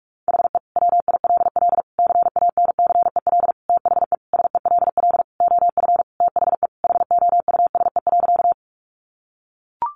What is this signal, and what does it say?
Morse code, heard as a keyed tone, about 44 wpm, 700 Hz. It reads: HE WILL CANCEL THE SELL OF THE HOUSE.
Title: Sentences from Top 1200 Words 45wpm